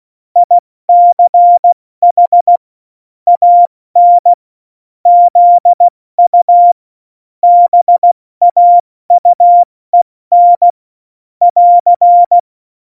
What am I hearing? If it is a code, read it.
Morse code, heard as a keyed tone, about 16 wpm, 700 Hz. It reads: ICH AN ZU BAUEN <AR>
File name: Morsekurs / Witze / Witz 20. Februar 2018, 70 F80